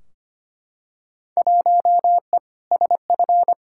Morse code, heard as a keyed tone, about 25 wpm, 700 Hz. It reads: 1E SF
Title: Field Day Exchanges - 25wpm